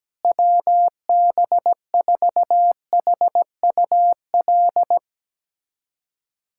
Morse code, heard as a keyed tone, about 17 wpm, 700 Hz. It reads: WB4HUL
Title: USA Callsign DE Callsign 17wpm